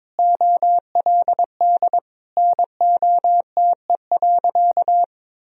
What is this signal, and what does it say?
Morse code, heard as a keyed tone, about 22 wpm, 700 Hz. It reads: OLD NOTE.
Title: Sentences from Top 1000 Words - Review Part 2 - 22wpm